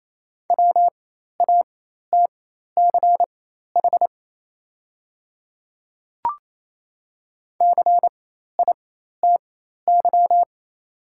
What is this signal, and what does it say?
Morse code, heard as a keyed tone, about 28 wpm, spaced as for 13 wpm, 700 Hz. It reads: WATCH CITY
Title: 4x Character Spacing - Top 500 Words - 28wpm